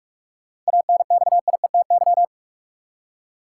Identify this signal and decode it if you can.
Morse code, heard as a keyed tone, about 46 wpm, 700 Hz. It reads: ANXIETY
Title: Seven Letter Words - 45wpm